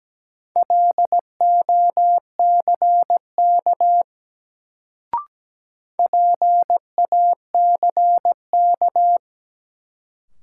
Morse code, heard as a keyed tone, about 17 wpm, 700 Hz. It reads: LOCK PACK
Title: Sentences from Top 1300 Words - New Word Drill - 17wpm